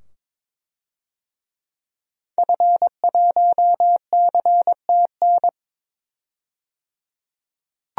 Morse code, heard as a keyed tone, about 21 wpm, 700 Hz. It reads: F1CTN
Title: DXCC-100 Call Signs - 5 Characters - 22wpm